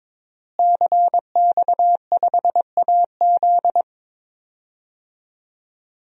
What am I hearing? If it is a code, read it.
Morse code, heard as a keyed tone, about 23 wpm, 700 Hz. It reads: CX5AZ
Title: DXCC-100 Call Signs - 5 Characters - Repeated 3x - 22wpm